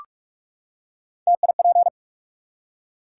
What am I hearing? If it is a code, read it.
Morse code, heard as a keyed tone, about 44 wpm, 700 Hz. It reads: TIP
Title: Three Letter Words - 45wpm